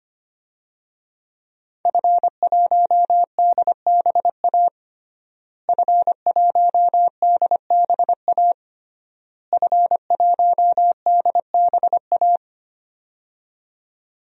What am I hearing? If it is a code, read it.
Morse code, heard as a keyed tone, about 25 wpm, 700 Hz. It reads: F1DBA F1DBA F1DBA
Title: DXCC-100 Call Signs - 5 Characters - Repeated 3x - 25wpm